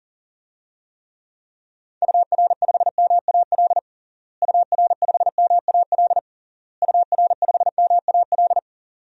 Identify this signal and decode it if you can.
Morse code, heard as a keyed tone, about 40 wpm, 700 Hz. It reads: UR5MAL UR5MAL UR5MAL
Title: DXCC-100 Call Signs - 6 Characters - Repeated 3x - 40wpm